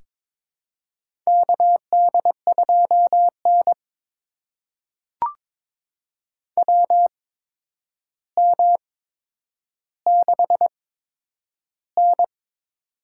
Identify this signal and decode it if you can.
Morse code, heard as a keyed tone, about 22 wpm, spaced as for 9 wpm, 700 Hz. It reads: KD2N WM6N
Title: USA Call Signs - 4 Characters - 8x Character Spacing - 22wpm